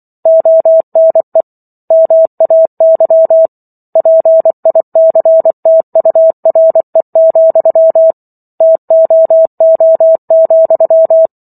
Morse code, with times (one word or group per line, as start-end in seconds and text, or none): 0.3-1.4 ONE
1.9-3.5 MAY
4.0-8.1 PICTURE,
8.6-11.3 TOO,